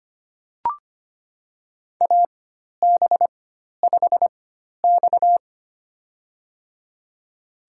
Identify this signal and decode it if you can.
Morse code, heard as a keyed tone, about 25 wpm, spaced as for 12 wpm, 700 Hz. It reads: AB5X
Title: USA Call Signs - 4 Characters - 4x Character Spacing - 25wpm